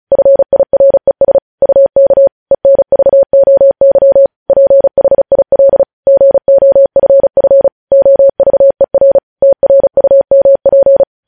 Morse code, heard as a keyed tone, about 36 wpm, 550 Hz. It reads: FIRES UK ENVOY PHIL GOFF OVER TRUMP
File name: podcast / News headlines-1x-35x35